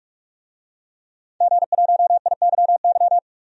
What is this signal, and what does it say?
Morse code, heard as a keyed tone, about 46 wpm, 700 Hz. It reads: G1IYY
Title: DXCC-100 Call Signs - 5 Characters - 45wpm